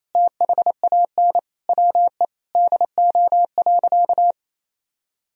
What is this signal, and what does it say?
Morse code, heard as a keyed tone, about 28 wpm, 700 Hz. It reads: THAN WE DO.
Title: Sentences from Top 100 Words and Abbreviations 28wpm